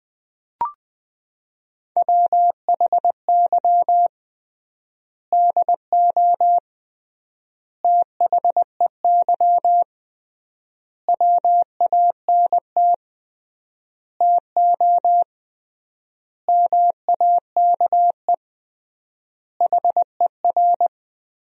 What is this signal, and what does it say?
Morse code, heard as a keyed tone, about 20 wpm, 700 Hz. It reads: WHY DO THEY WANT TO MAKE HER
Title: 3x Word-Spacing - Sentences from Top 400 Words - 20wpm